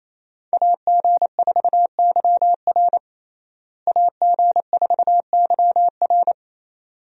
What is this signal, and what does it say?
Morse code, heard as a keyed tone, about 28 wpm, 700 Hz. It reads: AG4YR AG4YR
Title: USA Call Signs - 5 Characters - Repeated 2x - 28wpm